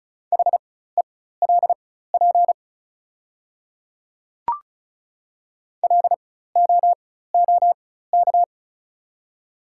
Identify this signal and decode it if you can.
Morse code, heard as a keyed tone, about 36 wpm, spaced as for 16 wpm, 700 Hz. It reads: HELP LOOK
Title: 4x Character Spacing - Top 300 Words - 35wpm